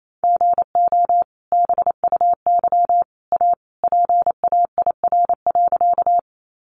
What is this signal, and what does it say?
Morse code, heard as a keyed tone, about 28 wpm, 700 Hz. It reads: GO BUY A PAIR.
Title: Sentences from Top 1200 Words 28wpm